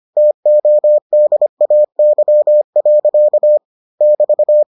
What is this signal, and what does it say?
Morse code, heard as a keyed tone, about 25 wpm, 600 Hz. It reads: TODAY. <BT>